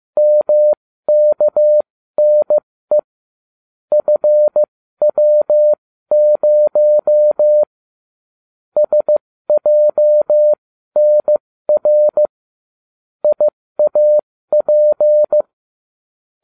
Morse code, heard as a keyed tone, about 15 wpm, spaced as for 13 wpm, 600 Hz. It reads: MKNE FW0 SJNR IAP